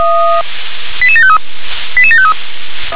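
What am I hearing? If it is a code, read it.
Five-tone selective call (ZVEI), heard as quick runs of five tones, tones of about 70 ms. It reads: 88753 88753
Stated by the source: ZVEI five-tone call received by SDR.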